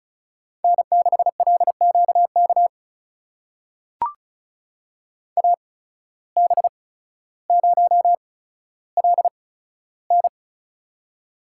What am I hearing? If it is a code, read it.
Morse code, heard as a keyed tone, about 36 wpm, spaced as for 15 wpm, 700 Hz. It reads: N6LQK AB0LN
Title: USA Call Signs - 5 Characters - 8x Character Spacing - 35wpm